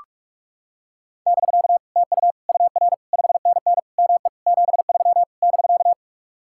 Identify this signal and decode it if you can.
Morse code, heard as a keyed tone, about 39 wpm, 700 Hz. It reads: <BK> TU UR 5NN ME 73 <BK>